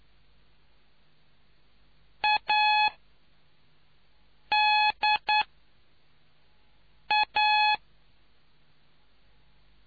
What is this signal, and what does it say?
Morse code, heard as a keyed tone, about 9 wpm, 850 Hz. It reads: A D A